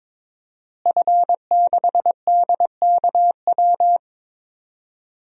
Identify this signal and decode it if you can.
Morse code, heard as a keyed tone, about 22 wpm, 700 Hz. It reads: F6DKW
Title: DXCC-100 Call Signs - 5 Characters - Repeated 3x - 22wpm